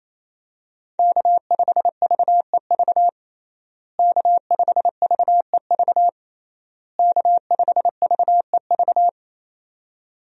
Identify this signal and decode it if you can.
Morse code, heard as a keyed tone, about 27 wpm, 700 Hz. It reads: K5VEV K5VEV K5VEV
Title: USA Call Signs - 5 Characters - Repeated 3x - 28wpm